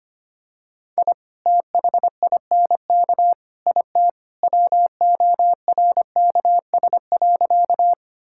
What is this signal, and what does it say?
Morse code, heard as a keyed tone, about 25 wpm, 700 Hz. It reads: I THINK IT WORKS.
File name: practice / 25wpm / Sentences with 2-Word Phrases - 25wpm